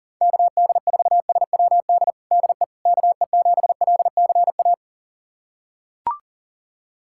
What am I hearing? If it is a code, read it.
Morse code, heard as a keyed tone, about 40 wpm, 700 Hz. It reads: KD4SWD DE KE7LCA
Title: USA Callsign DE Callsign - 6 Characters - 40wpm